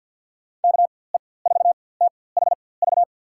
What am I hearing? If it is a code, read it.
Morse code, heard as a keyed tone, about 53 wpm, spaced as for 24 wpm, 700 Hz. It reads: KE4THV
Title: USA Call Signs - 6 Characters - 4x Character Spacing - 50wpm